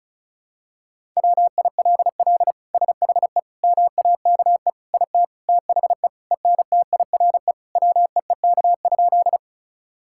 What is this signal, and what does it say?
Morse code, heard as a keyed tone, about 36 wpm, 700 Hz. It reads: WILL SHE MAKE IT THE ENTIRE WEEK?
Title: Sentences from Top 1000 Words - Review Part 5 - 35wpm